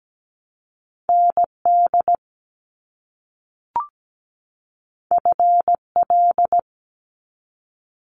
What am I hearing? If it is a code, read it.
Morse code, heard as a keyed tone, about 17 wpm, 700 Hz. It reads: ND FL
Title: US States 17wpm